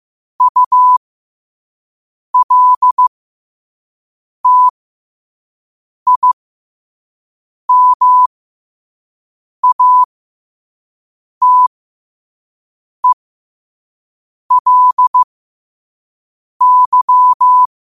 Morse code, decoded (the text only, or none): ULTIMATELY